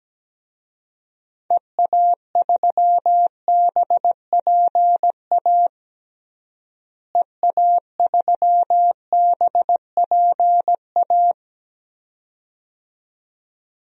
Morse code, heard as a keyed tone, about 17 wpm, 700 Hz. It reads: EA3BPA EA3BPA
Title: DXCC-100 Call Signs - 6 Characters - Repeated 2x - 17wpm